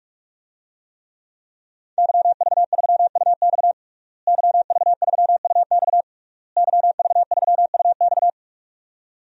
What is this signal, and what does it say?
Morse code, heard as a keyed tone, about 46 wpm, 700 Hz. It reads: YV3UX YV3UX YV3UX
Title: DXCC-100 Call Signs - 5 Characters - Repeated 3x - 45wpm